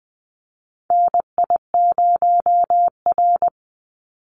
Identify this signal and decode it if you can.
Morse code, heard as a keyed tone, about 19 wpm, 700 Hz. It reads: NI0R
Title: Call Signs - 20wpm